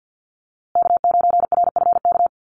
Morse code, heard as a keyed tone, about 50 wpm, 700 Hz. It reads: K9RFK
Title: USA Call Signs Repeated 3x 50wpm